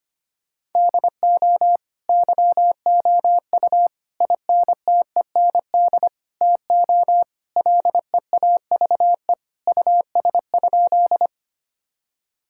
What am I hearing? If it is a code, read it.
Morse code, heard as a keyed tone, about 25 wpm, 700 Hz. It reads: DO YOU INTEND TO LEAVE US?
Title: Sentences from Top 1600 Words 25wpm